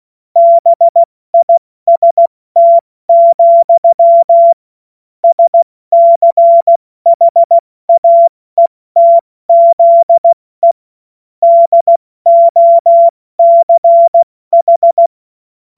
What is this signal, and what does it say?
Morse code, heard as a keyed tone, about 16 wpm, spaced as for 14 wpm, 680 Hz. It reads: BIST, SCHAETZE DOCH
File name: Morsekurs / Witze / Witz 14. September 2021, 70 F80